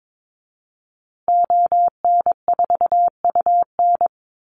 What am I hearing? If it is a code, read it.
Morse code, heard as a keyed tone, about 22 wpm, 700 Hz. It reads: ON4UN
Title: Call Signs Repeated 3x - 22wpm